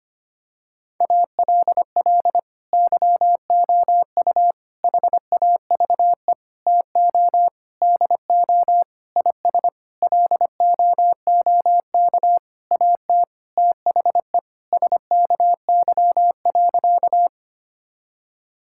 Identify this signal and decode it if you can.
Morse code, heard as a keyed tone, about 25 wpm, 700 Hz. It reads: ALL YOU HAVE TO DO IS LOOK AT THE SKY.
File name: practice / 25wpm / Sentences with 9-Word Phrases - 25wpm